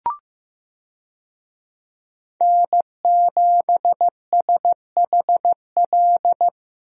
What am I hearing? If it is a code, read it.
Morse code, heard as a keyed tone, about 15 wpm, 700 Hz. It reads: N7SHL
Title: USA Call Signs - 5 Characters - Repeated 2x - 15wpm